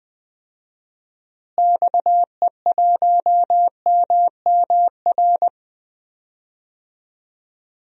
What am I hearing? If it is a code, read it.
Morse code, heard as a keyed tone, about 20 wpm, 700 Hz. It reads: XE1MMR
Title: DXCC-100 Call Signs - 6 Characters - Repeated 3x - 20wpm